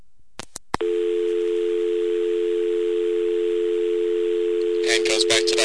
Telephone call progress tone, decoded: dial tone